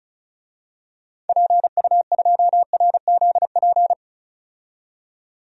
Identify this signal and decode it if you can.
Morse code, heard as a keyed tone, about 36 wpm, 700 Hz. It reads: PU2RZP
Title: DXCC-100 Call Signs - 6 Characters - 35wpm